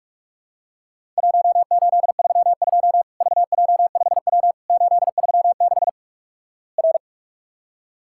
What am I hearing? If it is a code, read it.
Morse code, heard as a keyed tone, about 46 wpm, 700 Hz. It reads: 1832 VJ5W 836 R
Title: Worked All Europe DX Contest - QTC Exchange - 45wpm